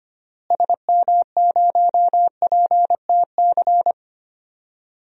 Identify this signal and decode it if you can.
Morse code, heard as a keyed tone, about 25 wpm, 700 Hz. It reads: SM0PTC
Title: DXCC-100 Call Signs - 6 Characters - Repeated 3x - 25wpm